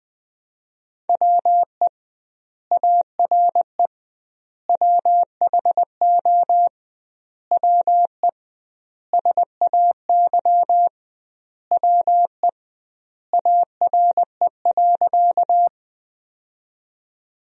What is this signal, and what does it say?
Morse code, heard as a keyed tone, about 20 wpm, 700 Hz. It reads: WE ARE WHO WE SAY WE ARE.
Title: 2x Word-Spacing - Sentences from Top 200 Words - 20wpm